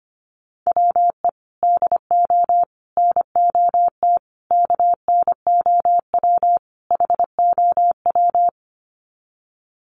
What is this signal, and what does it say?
Morse code, heard as a keyed tone, about 25 wpm, 700 Hz. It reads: WE DO NOT KNOW HOW
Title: Sets of 5 Words Encore 25wpm